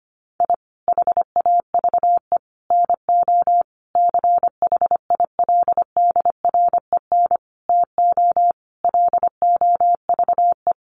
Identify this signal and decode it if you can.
Morse code, heard as a keyed tone, about 25 wpm, 700 Hz. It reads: I HAVE NO CHILDREN TO LOVE
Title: Sentences from Top 400 Words 25wpm